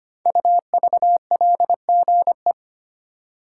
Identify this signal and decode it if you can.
Morse code, heard as a keyed tone, about 25 wpm, 700 Hz. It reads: UVLGE
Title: Code Groups - 5 Letter:Numbers - 25wpm